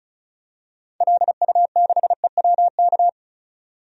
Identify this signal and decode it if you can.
Morse code, heard as a keyed tone, about 36 wpm, 700 Hz. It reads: LU6EWK
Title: DXCC-100 Call Signs - 6 Characters - 35wpm